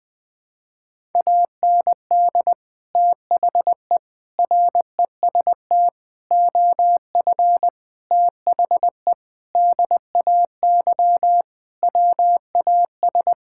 Morse code, heard as a keyed tone, about 20 wpm, 700 Hz. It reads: AND THE REST OF THE DAY WAS
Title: Sentences with 4-Word Phrases - 20wpm